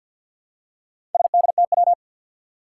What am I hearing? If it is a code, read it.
Morse code, heard as a keyed tone, about 49 wpm, 700 Hz. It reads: SDTW
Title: Four Challenging Letter-Number Combinations 50wpm